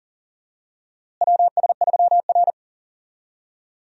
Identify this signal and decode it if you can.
Morse code, heard as a keyed tone, about 40 wpm, 700 Hz. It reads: WS3R
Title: USA Call Signs - 4 Characters - Repeated 3x - 40wpm